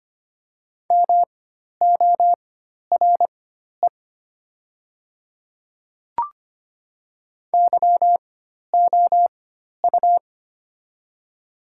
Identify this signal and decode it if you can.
Morse code, heard as a keyed tone, about 25 wpm, spaced as for 12 wpm, 700 Hz. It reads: MORE YOU
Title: 4x Character Spacing - Top 100 Words - 25wpm